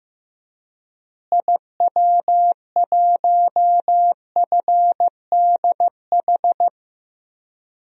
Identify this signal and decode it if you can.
Morse code, heard as a keyed tone, about 15 wpm, 700 Hz. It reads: IW1FDH
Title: DXCC-100 Call Signs - 6 Characters - Repeated 3x - 15wpm